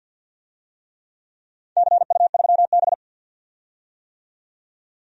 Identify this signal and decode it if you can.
Morse code, heard as a keyed tone, about 50 wpm, 700 Hz. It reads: CU3B
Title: DXCC-100 Call Signs - 4 Characters - 50wpm